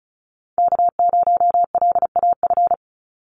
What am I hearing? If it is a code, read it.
Morse code, heard as a keyed tone, about 34 wpm, 700 Hz. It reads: K0LAF
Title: NA SOTA Chaser Callsigns 35wpm